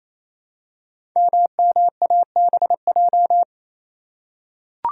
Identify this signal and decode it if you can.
Morse code, heard as a keyed tone, about 28 wpm, 700 Hz. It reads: MMABJ
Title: Code Groups - 5 Letter:Numbers - 28wpm